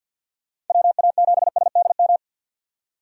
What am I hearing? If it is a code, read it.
Morse code, heard as a keyed tone, about 51 wpm, 700 Hz. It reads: WA7SDM
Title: USA Call Signs - 4x Character Spacing - 50wpm